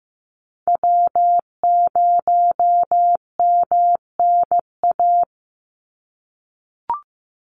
Morse code, heard as a keyed tone, about 15 wpm, 700 Hz. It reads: W0MNA
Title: NA SOTA Chaser Callsigns 15wpm